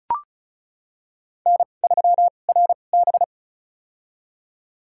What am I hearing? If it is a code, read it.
Morse code, heard as a keyed tone, about 36 wpm, spaced as for 25 wpm, 700 Hz. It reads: N3RB
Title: USA Call Signs - 2x Character Spacing - 35wpm